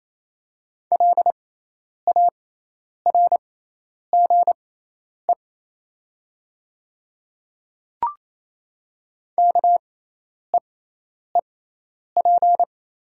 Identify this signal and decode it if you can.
Morse code, heard as a keyed tone, about 28 wpm, spaced as for 10 wpm, 700 Hz. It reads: LARGE KEEP
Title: 6x Character Spacing - Top 300 Words - 28wpm